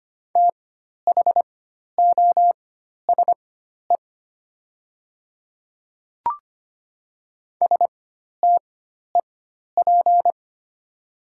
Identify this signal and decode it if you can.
Morse code, heard as a keyed tone, about 25 wpm, spaced as for 12 wpm, 690 Hz. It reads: THOSE STEP
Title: 4x Character Spacing - Top 500 Words - 25wpm